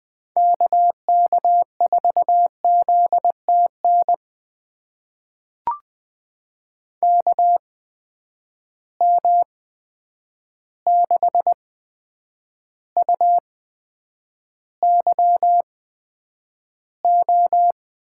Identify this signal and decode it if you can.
Morse code, heard as a keyed tone, about 20 wpm, spaced as for 9 wpm, 700 Hz. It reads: KK4ZTN KM6UYO
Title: USA Call Signs - 6 Characters - 8x Character Spacing - 20wpm